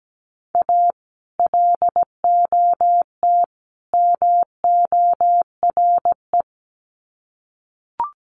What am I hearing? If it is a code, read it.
Morse code, heard as a keyed tone, about 17 wpm, 700 Hz. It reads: A LOT MORE
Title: Sets of 3 Words 17wpm